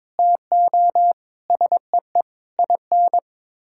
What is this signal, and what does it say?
Morse code, heard as a keyed tone, about 22 wpm, 700 Hz. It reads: TO SEE IN
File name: practice / 22wpm / Sets of 8 Words - 22wpm